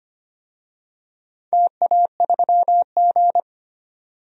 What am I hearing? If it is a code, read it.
Morse code, heard as a keyed tone, about 25 wpm, 700 Hz. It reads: TA3G
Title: DXCC-100 Call Signs - 4 Characters - 25wpm